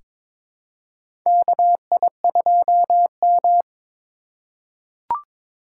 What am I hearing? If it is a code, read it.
Morse code, heard as a keyed tone, about 22 wpm, 700 Hz. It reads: KI2M
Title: USA Call Signs - 4 Characters - 2x Character Spacing - 22wpm